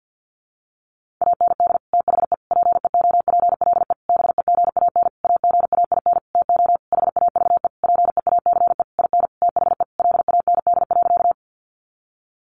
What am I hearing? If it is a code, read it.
Morse code, heard as a keyed tone, about 48 wpm, 700 Hz. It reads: AND THE PEOPLE BEGAN AGAIN TO HAVE PEACE IN THE LAND.
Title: Sentences from Top 1500 Words 50wpm